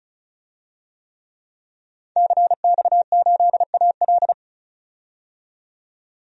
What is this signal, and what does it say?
Morse code, heard as a keyed tone, about 36 wpm, 700 Hz. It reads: CX8AL
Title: DXCC-100 Call Signs - 5 Characters - 35wpm